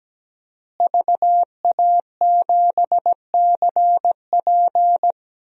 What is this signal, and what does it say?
Morse code, heard as a keyed tone, about 17 wpm, 700 Hz. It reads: VA7CP
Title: DXCC-100 Call Signs - 5 Characters - Repeated 3x - 17wpm